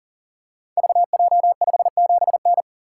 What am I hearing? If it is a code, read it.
Morse code, heard as a keyed tone, about 43 wpm, 700 Hz. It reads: VJ57N
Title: Code Groups - 5 Letter:Numbers - 40wpm